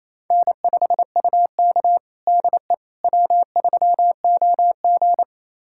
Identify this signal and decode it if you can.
Morse code, heard as a keyed tone, about 28 wpm, 700 Hz. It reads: N5UK DE W3OG
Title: USA Callsign DE Callsign - 4 Characters - 28wpm